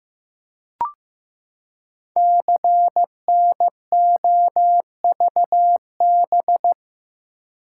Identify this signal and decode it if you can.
Morse code, heard as a keyed tone, about 15 wpm, 700 Hz. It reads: CNOVB
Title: Code Groups - 5 Letters - 15wpm